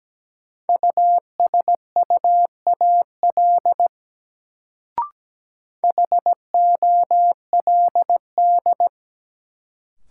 Morse code, heard as a keyed tone, about 17 wpm, 700 Hz. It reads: USUAL HOLD
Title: Sentences from Top 400 Words - New Word Drill - 17wpm